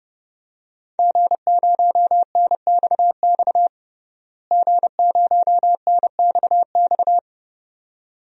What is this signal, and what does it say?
Morse code, heard as a keyed tone, about 30 wpm, 700 Hz. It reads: G0NXX G0NXX
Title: DXCC-100 Call Signs - 5 Characters - Repeated 2x - 30wpm